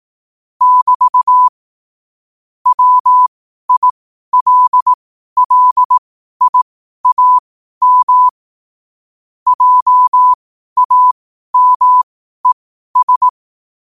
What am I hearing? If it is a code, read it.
Morse code, heard as a keyed tone, about 18 wpm, spaced as for 12 wpm, 1000 Hz. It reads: <BT> WILLIAM JAMES
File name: qotdfiles / 12wpm